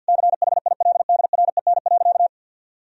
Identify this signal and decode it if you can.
Morse code, heard as a keyed tone, about 50 wpm, 700 Hz. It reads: CHILDREN.
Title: Sentences from Top 1000 Words - Review Part 4 - 50wpm